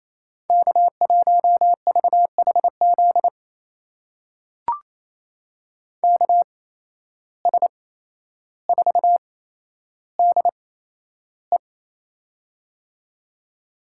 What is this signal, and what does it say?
Morse code, heard as a keyed tone, about 28 wpm, spaced as for 12 wpm, 700 Hz. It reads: K1VHZ KS4DE
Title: USA Call Signs - 5 Characters - 8x Character Spacing - 28wpm